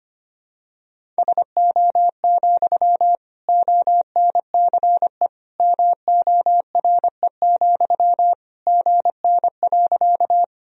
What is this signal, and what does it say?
Morse code, heard as a keyed tone, about 25 wpm, 700 Hz. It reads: SO, ONCE MORE, GN.
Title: Sentences from Top 300 Words and Abbreviations 25wpm